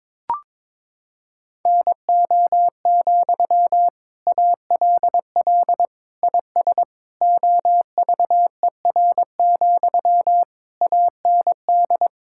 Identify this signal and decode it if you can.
Morse code, heard as a keyed tone, about 22 wpm, 700 Hz. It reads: NO, ALL IS OVER, AND
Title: Sentences from Top 1600 Words 22wpm